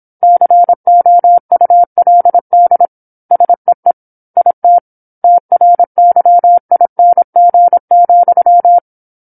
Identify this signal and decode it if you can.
Morse code, heard as a keyed tone, about 27 wpm, 700 Hz. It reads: COULD SEE IT TRYING,
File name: 08-Chapter-08 26 WPM